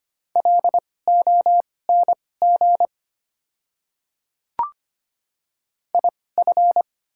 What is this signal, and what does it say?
Morse code, heard as a keyed tone, about 25 wpm, spaced as for 18 wpm, 690 Hz. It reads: LONG IF
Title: 2x Character Spacing - Top 100 Words - 25wpm